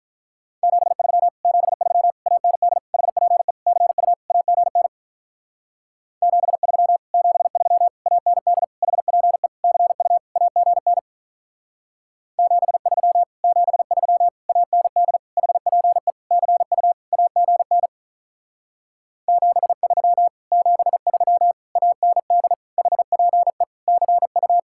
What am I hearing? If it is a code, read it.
Morse code, heard as a keyed tone, about 46 wpm, 700 Hz. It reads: 73 73 AND HPE CU AGN 73 73 AND HPE CU AGN 73 73 AND HPE CU AGN 73 73 AND HPE CU